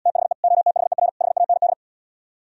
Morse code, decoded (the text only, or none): THE OTHER HAND